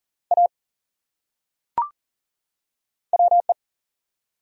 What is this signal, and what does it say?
Morse code, heard as a keyed tone, about 40 wpm, 700 Hz. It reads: A WE
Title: Top 100 Words 40wpm